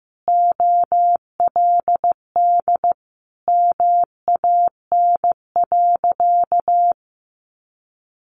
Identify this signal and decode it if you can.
Morse code, heard as a keyed tone, about 15 wpm, 700 Hz. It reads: OLD MAN.
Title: Sentences from Top 500 Words 15wpm